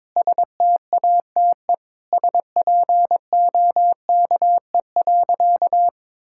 Morse code, heard as a keyed tone, about 22 wpm, 700 Hz. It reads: STATE SPOKE.
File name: practice / 22wpm / Sentences with 3-Word Phrases - 22wpm